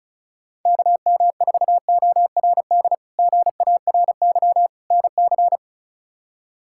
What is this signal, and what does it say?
Morse code, heard as a keyed tone, about 36 wpm, 700 Hz. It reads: KM4ORD GARY NC
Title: K1USN SST QSO Practice - 35wpm